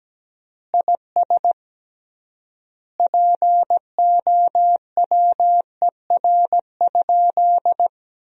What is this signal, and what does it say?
Morse code, heard as a keyed tone, about 17 wpm, 700 Hz. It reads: IS POWER?
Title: 3x Word-Spacing - Sentences from Top 500 Words - 17wpm